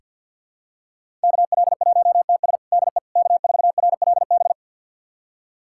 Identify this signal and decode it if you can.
Morse code, heard as a keyed tone, about 50 wpm, 700 Hz. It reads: KL1TS DE K4RLB
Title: USA Callsign DE Callsign 50wpm